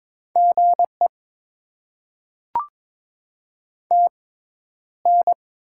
Morse code, heard as a keyed tone, about 22 wpm, spaced as for 11 wpm, 700 Hz. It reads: GE TN
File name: Top 20 QSO Elements - 6x Character Spacing - 22wpm